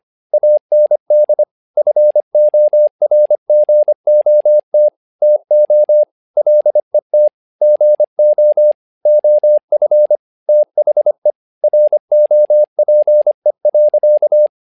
Morse code, heard as a keyed tone, about 25 wpm, 600 Hz. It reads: AND FORGOT TO LET GO OF THE ROPE.
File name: learning center / tower trouble / tower trouble 25wpm